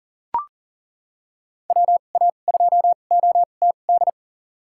Morse code, heard as a keyed tone, about 40 wpm, spaced as for 29 wpm, 700 Hz. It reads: WA2OTD
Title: USA Call Signs - 6 Characters - 2x Character Spacing - 40wpm